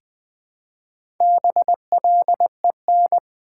Morse code, heard as a keyed tone, about 20 wpm, 700 Hz. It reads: BLEN
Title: Build A Word - 1 Letter At A Time - 3 to 5-Letter Words - 20wpm